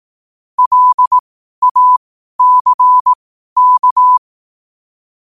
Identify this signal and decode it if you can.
Morse code, heard as a keyed tone, about 18 wpm, spaced as for 12 wpm, 1000 Hz. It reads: LACK
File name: qotdfiles / 12wpm